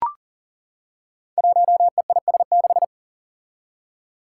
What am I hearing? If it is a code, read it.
Morse code, heard as a keyed tone, about 40 wpm, 700 Hz. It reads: 1EIS6
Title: Five Challenging Letter-Number Combinations 40wpm